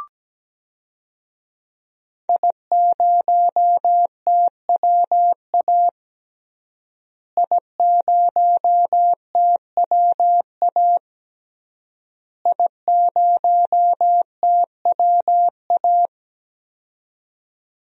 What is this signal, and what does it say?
Morse code, heard as a keyed tone, about 17 wpm, 700 Hz. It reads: I0TWA I0TWA I0TWA
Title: DXCC-100 Call Signs - 5 Characters - Repeated 3x - 17wpm